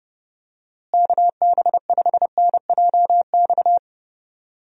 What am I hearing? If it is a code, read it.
Morse code, heard as a keyed tone, about 30 wpm, 700 Hz. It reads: KB5NJX
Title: USA Call Signs - 6 Characters - Repeated 2x - 30wpm